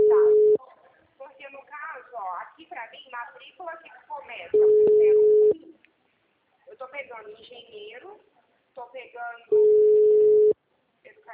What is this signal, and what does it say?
Ringback tone (Europe, 425 Hz), with speech.
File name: out- -2005- - - .51894